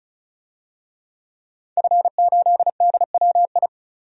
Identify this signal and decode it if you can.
Morse code, heard as a keyed tone, about 36 wpm, 700 Hz. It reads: F8DWI
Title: DXCC-100 Call Signs - 5 Characters - Repeated 3x - 35wpm